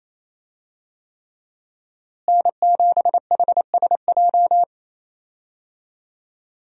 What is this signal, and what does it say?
Morse code, heard as a keyed tone, about 28 wpm, 700 Hz. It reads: N7HSJ
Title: USA Call Signs - 5 Characters - Repeated 3x - 28wpm